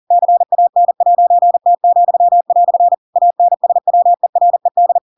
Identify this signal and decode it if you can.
Morse code, heard as a keyed tone, about 40 wpm, 700 Hz. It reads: CAN'T," ANSWERED
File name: Pinocchio / Pinocchio ch7 40 40 ik1hge